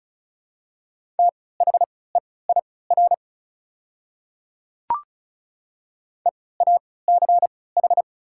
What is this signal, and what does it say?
Morse code, heard as a keyed tone, about 34 wpm, spaced as for 20 wpm, 690 Hz. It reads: THEIR EACH
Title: 3x Character Spacing - Top 100 Words - 35wpm